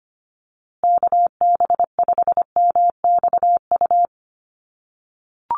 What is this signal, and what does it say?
Morse code, heard as a keyed tone, about 25 wpm, 700 Hz. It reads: KB5MXU
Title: USA Call Signs Repeated 2x 25wpm